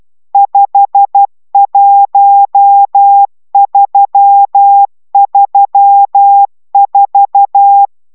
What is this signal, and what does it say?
Morse code, heard as a keyed tone, about 12 wpm, 800 Hz. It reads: 51334